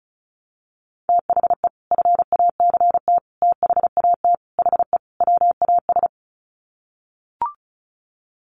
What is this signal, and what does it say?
Morse code, heard as a keyed tone, about 34 wpm, 700 Hz. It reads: THE FACT THAT HE WAS
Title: Sets of 5 Words 35wpm